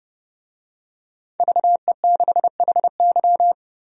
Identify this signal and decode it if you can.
Morse code, heard as a keyed tone, about 32 wpm, 700 Hz. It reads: VE6HY
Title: DXCC-100 Call Signs - 5 Characters - 30wpm